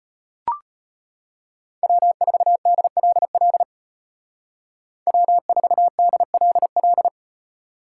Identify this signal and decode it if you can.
Morse code, heard as a keyed tone, about 33 wpm, 700 Hz. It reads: W4DLL W4DLL